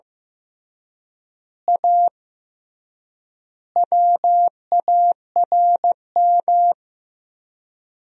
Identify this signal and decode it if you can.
Morse code, heard as a keyed tone, about 15 wpm, 700 Hz. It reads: A WARM
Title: 3x Word-Spacing - Sentences from Top 500 Words - 15wpm